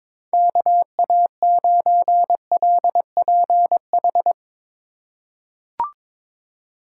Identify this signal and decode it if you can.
Morse code, heard as a keyed tone, about 22 wpm, 700 Hz. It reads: KA9LPH
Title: USA Call Signs - 6 Characters - 2x Character Spacing - 22wpm